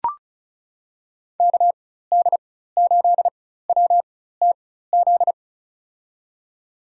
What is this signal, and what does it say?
Morse code, heard as a keyed tone, about 36 wpm, spaced as for 16 wpm, 700 Hz. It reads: KD8WTZ
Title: USA Call Signs - 6 Characters - 4x Character Spacing - 35wpm